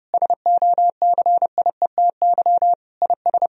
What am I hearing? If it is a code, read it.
Morse code, heard as a keyed tone, about 32 wpm, 700 Hz. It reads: SOCIETY IS
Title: Sentences from Top 1000 Words - Review Part 4 - 30wpm